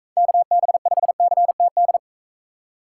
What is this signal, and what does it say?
Morse code, heard as a keyed tone, about 42 wpm, 700 Hz. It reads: KB5CTD